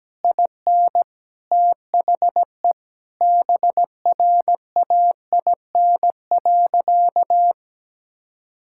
Morse code, heard as a keyed tone, about 17 wpm, 700 Hz. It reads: IN THE BRAIN.
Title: Sentences from Top 1000 Words - Review Part 6 - 17wpm